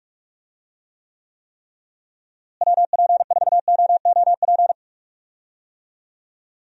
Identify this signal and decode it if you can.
Morse code, heard as a keyed tone, about 46 wpm, 700 Hz. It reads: WP4OOP
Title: USA Call Signs - 6 Characters - Repeated 3x - 45wpm